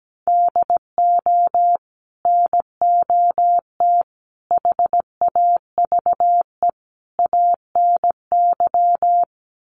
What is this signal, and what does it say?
Morse code, heard as a keyed tone, about 17 wpm, 700 Hz. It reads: DO NOT HAVE ANY
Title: Sets of 4 Words 17wpm